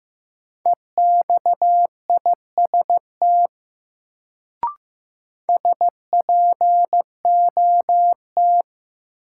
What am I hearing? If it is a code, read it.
Morse code, heard as a keyed tone, about 15 wpm, 700 Hz. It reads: EXIST SPOT
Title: Sentences from Top 1100 Words - New Word Drill Alt - 15wpm